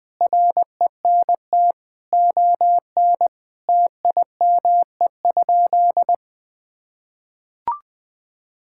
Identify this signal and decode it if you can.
Morse code, heard as a keyed tone, about 20 wpm, 700 Hz. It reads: RENT ON TIME?
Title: Sentences from Top 1000 Words - Review Part 5 - 20wpm